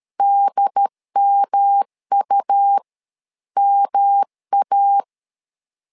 Morse code, heard as a keyed tone, about 13 wpm, spaced as for 11 wpm, 800 Hz. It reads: DMUMA
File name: LEZIONI / 3 MWRO / 3 M 12wpm 1541